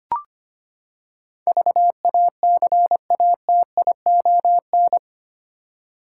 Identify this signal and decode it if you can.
Morse code, heard as a keyed tone, about 25 wpm, 700 Hz. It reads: VACATION
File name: Eight Letter Words - 25wpm